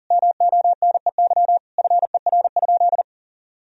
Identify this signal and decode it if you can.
Morse code, heard as a keyed tone, about 40 wpm, 700 Hz. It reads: MONEY FER?
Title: Sentences from Top 400 Words and Abbreviations 40wpm